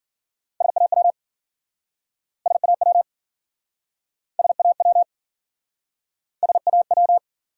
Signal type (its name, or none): Morse code